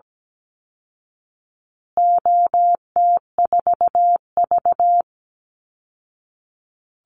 Morse code, heard as a keyed tone, about 17 wpm, 700 Hz. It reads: OT4V
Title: Call Signs Repeated 3x - 17wpm